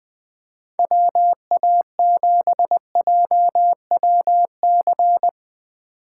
Morse code, heard as a keyed tone, about 19 wpm, 700 Hz. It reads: WA7JWC
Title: Speed Racing - USA 2x3 Call Signs 17wpm